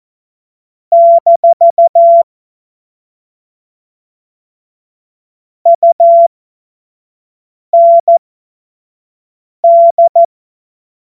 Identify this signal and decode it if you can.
Morse code, heard as a keyed tone, about 14 wpm, spaced as for 5 wpm, 680 Hz. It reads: - UND
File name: Morsekurs / Witze / Witz 08. November 2022 25 F70